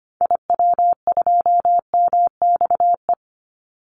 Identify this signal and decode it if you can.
Morse code, heard as a keyed tone, about 25 wpm, 700 Hz. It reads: IW2MXE
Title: Call Signs Repeated 2x - 25wpm